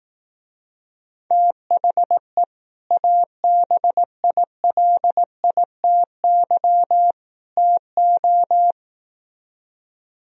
Morse code, heard as a keyed tone, about 18 wpm, 700 Hz. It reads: THE ABILITY TO
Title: Speed-Racing - Sets of 3 Words 15wpm